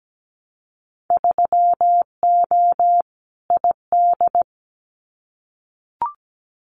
Morse code, heard as a keyed tone, about 17 wpm, 700 Hz. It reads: 3O ID
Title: Winter Field Day Exchanges Encore 17wpm